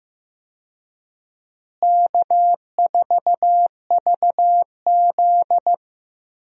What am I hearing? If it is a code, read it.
Morse code, heard as a keyed tone, about 15 wpm, 700 Hz. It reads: K4VZ
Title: USA Call Signs - 4 Characters - Repeated 3x - 15wpm